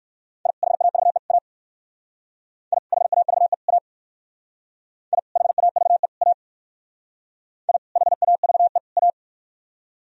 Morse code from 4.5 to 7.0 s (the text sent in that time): I HAVE A